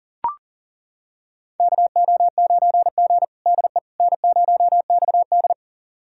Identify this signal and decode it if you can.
Morse code, heard as a keyed tone, about 40 wpm, 700 Hz. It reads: KO9G DE N0XD
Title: USA Callsign DE Callsign - 4 Characters - 40wpm